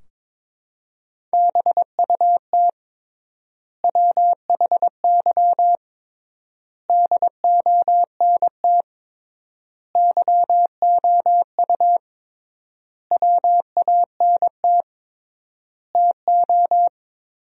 Morse code, heard as a keyed tone, about 22 wpm, 700 Hz. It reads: BUT WHY DONT YOU WANT TO
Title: 3x Word-Spacing - Sentences from Top 300 Words - 22wpm